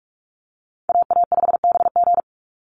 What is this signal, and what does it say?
Morse code, heard as a keyed tone, about 44 wpm, 700 Hz. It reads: AA5BG
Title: Call Signs - 45wpm